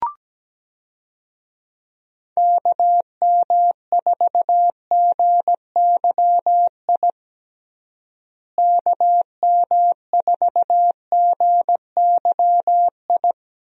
Morse code, heard as a keyed tone, about 17 wpm, 700 Hz. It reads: KM4GYI KM4GYI
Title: USA Call Signs - 6 Characters - Repeated 2x - 17wpm